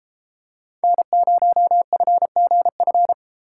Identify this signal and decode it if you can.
Morse code, heard as a keyed tone, about 31 wpm, 700 Hz. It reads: N0FGF